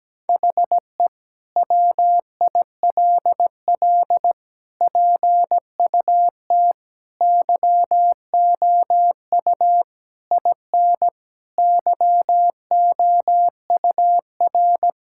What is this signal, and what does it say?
Morse code, heard as a keyed tone, about 17 wpm, 700 Hz. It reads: HE WILL PUT YOU IN YOUR
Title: Sentences from Top 1000 Words - Review Part 5 - 17wpm